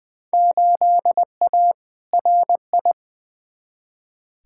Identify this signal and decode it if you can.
Morse code, heard as a keyed tone, about 20 wpm, 700 Hz. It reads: 8A RI
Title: Field Day Exchanges Encore - 20wpm